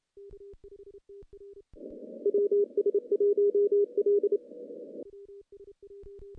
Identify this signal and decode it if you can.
Morse code, heard as a keyed tone, about 28 wpm, 410 Hz. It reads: WS1L
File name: contests / CW-SS-k5tr- -0000